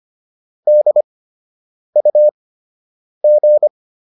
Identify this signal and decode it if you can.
Morse code, heard as a keyed tone, about 25 wpm, spaced as for 8 wpm, 600 Hz. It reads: DUG